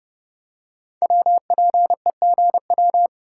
Morse code, heard as a keyed tone, about 30 wpm, 700 Hz. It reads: WPEGW
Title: Code Groups - 5 Letter:Numbers - 30wpm